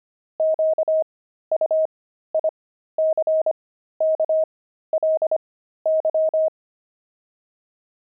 Morse code, heard as a keyed tone, about 25 wpm, spaced as for 13 wpm, 630 Hz. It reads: QUICKLY
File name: pangram-fs-13-cs-25-jumping